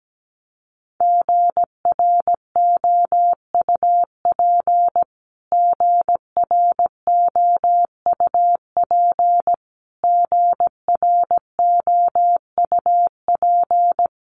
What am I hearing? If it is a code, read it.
Morse code, heard as a keyed tone, about 17 wpm, 700 Hz. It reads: GROUP GROUP GROUP
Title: Top 500 Words Repeated 17wpm